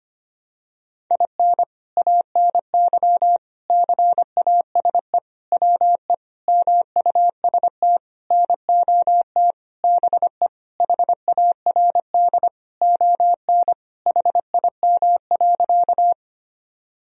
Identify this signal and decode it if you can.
Morse code, heard as a keyed tone, about 25 wpm, 700 Hz. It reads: IN ANY CASE WE MUST NOT BE HARD ON HIM.
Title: Sentences from Top 1000 Words - Review Part 1 - 25wpm